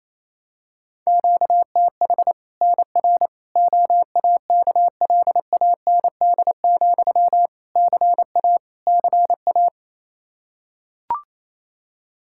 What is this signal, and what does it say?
Morse code, heard as a keyed tone, about 27 wpm, 700 Hz. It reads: QTH NR OAKLAND, CA CA
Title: K1USN SST QSO Practice 28wpm